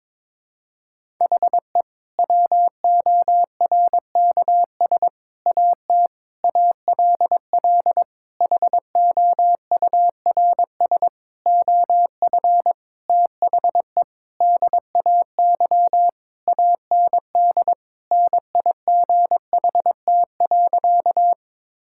Morse code, heard as a keyed tone, about 22 wpm, 700 Hz. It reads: HE WORKS AT ALL HOURS OF THE DAY AND NIGHT.
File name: practice / 22wpm / Sentences with 8-Word Phrases - 22wpm